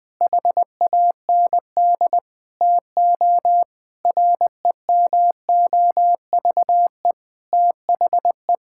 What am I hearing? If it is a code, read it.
Morse code, heard as a keyed tone, about 20 wpm, 700 Hz. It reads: HAND TO REMOVE THE
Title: Sentences from Top 1000 Words - Review Part 5 - 20wpm